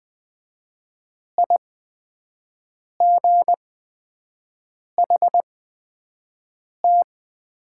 Morse code, heard as a keyed tone, about 20 wpm, spaced as for 5 wpm, 700 Hz. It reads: IGHT
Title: 8x Character Spacing - Top 300 Words - 20wpm